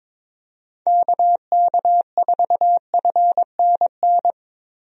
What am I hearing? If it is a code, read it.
Morse code, heard as a keyed tone, about 22 wpm, 700 Hz. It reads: KK4FNN
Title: USA Call Signs - 6 Characters - Repeated 2x - 22wpm